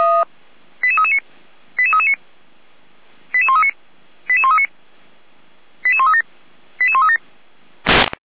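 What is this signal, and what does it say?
Five-tone selective call (ZVEI), heard as quick runs of five tones, tones of about 70 ms. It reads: 88339 88339 88139 88139 88137 88137
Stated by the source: ZVEI five-tone selective call received by SDR, logged by feuerwehr-orlamuende (fire brigade Orlamünde).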